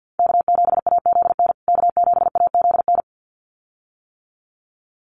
Morse code, heard as a keyed tone, about 50 wpm, 700 Hz. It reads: K7AZN K7AZN
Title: USA Call Signs Repeated 2x 50wpm